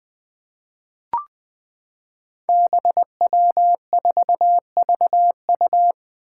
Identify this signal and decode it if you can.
Morse code, heard as a keyed tone, about 20 wpm, 700 Hz. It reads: BW4VU
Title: Five Challenging Letter-Number Combinations 20wpm